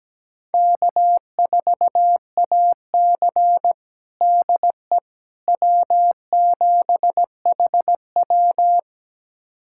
Morse code, heard as a keyed tone, about 17 wpm, 700 Hz. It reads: K4AC DE W7HW
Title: USA Callsign DE Callsign - 4 Characters - 17wpm